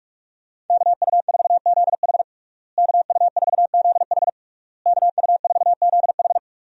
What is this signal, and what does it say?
Morse code, heard as a keyed tone, about 46 wpm, 700 Hz. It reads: KU4ZH KU4ZH KU4ZH
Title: USA Call Signs - 5 Characters - Repeated 3x - 45wpm